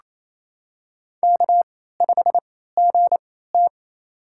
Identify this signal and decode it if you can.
Morse code, heard as a keyed tone, about 28 wpm, spaced as for 16 wpm, 700 Hz. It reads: K5GT
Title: USA Call Signs - 4 Characters - 3x Character Spacing - 28wpm